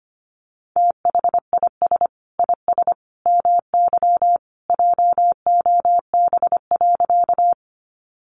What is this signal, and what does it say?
Morse code, heard as a keyed tone, about 25 wpm, 700 Hz. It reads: THIS IS MY JOB.
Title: Sentences from Top 600 Words 25wpm